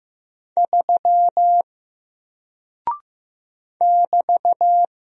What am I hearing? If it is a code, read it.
Morse code, heard as a keyed tone, about 15 wpm, 700 Hz. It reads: 3 <BT>
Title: Single Character (Letter:Number:Punctuation:Prosign) Alt 15wpm